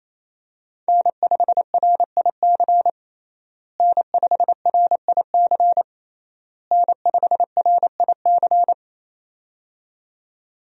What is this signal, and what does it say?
Morse code, heard as a keyed tone, about 28 wpm, 700 Hz. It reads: N5RIC N5RIC N5RIC
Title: USA Call Signs - 5 Characters - Repeated 3x - 28wpm